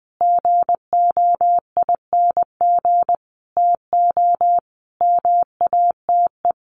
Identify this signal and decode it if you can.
Morse code, heard as a keyed tone, about 20 wpm, 700 Hz. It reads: GOING TO MATE
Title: A Sound Of Thunder 20wpm